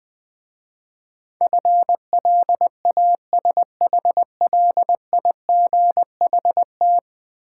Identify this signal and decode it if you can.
Morse code, heard as a keyed tone, about 21 wpm, 700 Hz. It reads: FLASHLIGHT
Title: Easy Words with Hints - 20wpm